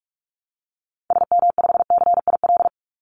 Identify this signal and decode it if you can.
Morse code, heard as a keyed tone, about 44 wpm, 700 Hz. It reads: SM5CIL
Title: Call Signs - 45wpm